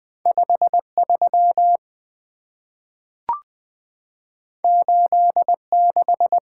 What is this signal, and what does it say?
Morse code, heard as a keyed tone, about 20 wpm, 700 Hz. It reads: 53 86
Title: Two Letter-Number Combinations - 20wpm